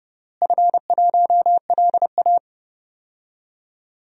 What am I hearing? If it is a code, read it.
Morse code, heard as a keyed tone, about 30 wpm, 700 Hz. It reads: F1LA
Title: DXCC-100 Call Signs - 4 Characters - Repeated 3x - 30wpm